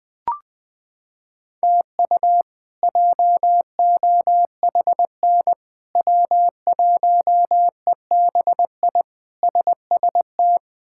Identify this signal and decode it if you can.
Morse code, heard as a keyed tone, about 20 wpm, 700 Hz. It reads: TU JOHN W1EBI SST
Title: K1USN SST QSO Practice - 20wpm